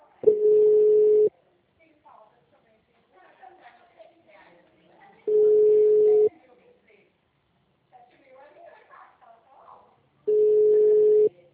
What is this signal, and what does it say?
Ringback tone (Europe, 425 Hz), with speech.